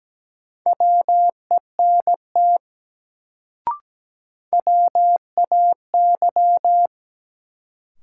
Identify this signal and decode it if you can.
Morse code, heard as a keyed tone, about 17 wpm, 700 Hz. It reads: WENT WAY
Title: Sets of 2 Words - New Word Drill - 17wpm